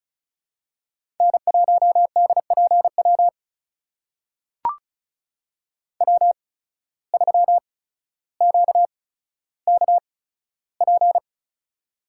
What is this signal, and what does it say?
Morse code, heard as a keyed tone, about 36 wpm, spaced as for 15 wpm, 700 Hz. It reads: N1DPW W3QKP
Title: USA Call Signs - 5 Characters - 8x Character Spacing - 35wpm